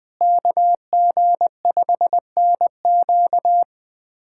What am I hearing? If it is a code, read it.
Morse code, heard as a keyed tone, about 21 wpm, 700 Hz. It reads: KG5NQ
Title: USA Callsign DE Callsign - 5 Characters - 20wpm